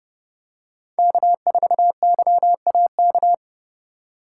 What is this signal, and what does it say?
Morse code, heard as a keyed tone, about 29 wpm, 700 Hz. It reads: K4YAK